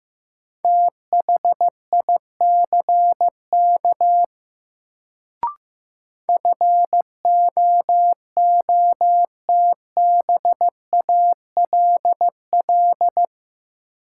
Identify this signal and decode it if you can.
Morse code, heard as a keyed tone, about 15 wpm, 700 Hz. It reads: THICK FOOTBALL
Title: Sentences from Top 1600 Words - New Word Drill - 15wpm